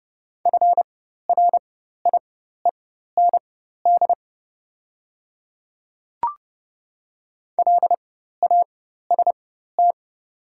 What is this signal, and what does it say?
Morse code, heard as a keyed tone, about 30 wpm, spaced as for 14 wpm, 700 Hz. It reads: FRIEND LAST
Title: 4x Character Spacing - Top 300 Words - 30wpm